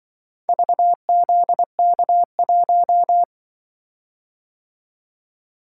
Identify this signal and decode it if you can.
Morse code, heard as a keyed tone, about 25 wpm, 700 Hz. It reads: VZK1
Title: Speed-Racing - Four Letter-Number Combinations - 20wpm